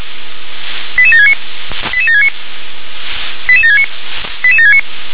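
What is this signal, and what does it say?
Five-tone selective call (ZVEI), heard as quick runs of five tones, tones of about 70 ms. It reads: 88769 88769 88769 88769
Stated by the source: ZVEI five-tone selective call 88766, received by SDR.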